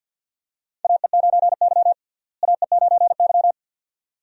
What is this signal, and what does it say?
Morse code, heard as a keyed tone, about 46 wpm, 700 Hz. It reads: AE9Y AE9Y
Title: USA Call Signs - 4 Characters - Repeated 2x - 50wpm